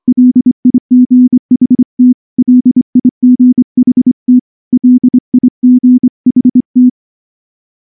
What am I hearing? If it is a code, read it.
Morse code, heard as a keyed tone, about 24 wpm, 250 Hz. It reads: LIGHT LIGHT LIGHT
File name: morse / morse-training-25WPM